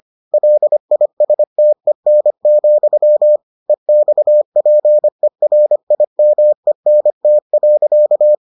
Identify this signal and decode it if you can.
Morse code, heard as a keyed tone, about 25 wpm, 600 Hz. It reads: LISTEN, EXPERIMENT.